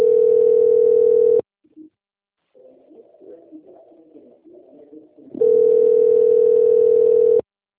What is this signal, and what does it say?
Ringback tone (North America), 440 + 480 Hz.
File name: out- - - - - .46489